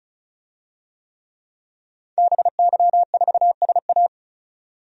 Morse code, heard as a keyed tone, about 34 wpm, 700 Hz. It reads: BY4SA
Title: DXCC-100 Call Signs - 5 Characters - 35wpm